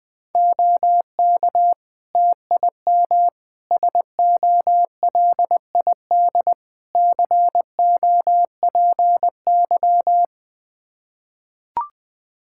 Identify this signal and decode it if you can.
Morse code, heard as a keyed tone, about 20 wpm, 700 Hz. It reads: OK TIM SOLID COPY
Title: Practice QSO - Exchange 3 - More Info - 20wpm